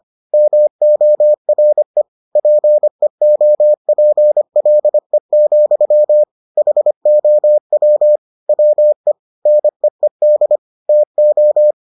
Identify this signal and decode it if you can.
Morse code, heard as a keyed tone, about 25 wpm, 600 Hz. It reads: MORE PEOPLE, HOW WE NEED TO